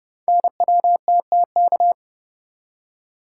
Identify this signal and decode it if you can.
Morse code, heard as a keyed tone, about 29 wpm, 700 Hz. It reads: NWTTK
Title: Code Groups - 5 Letters - 30wpm